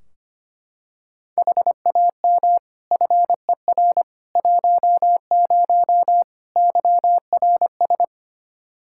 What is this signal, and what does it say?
Morse code, heard as a keyed tone, about 25 wpm, 700 Hz. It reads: HAM FER 10 YRS
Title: K1USN SST QSO Practice 25wpm